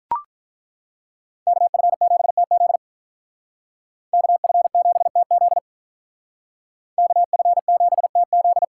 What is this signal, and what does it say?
Morse code, heard as a keyed tone, about 42 wpm, 700 Hz. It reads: KF7TZ KF7TZ KF7TZ
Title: Speed Racing - USA 2x2 Call Signs 35wpm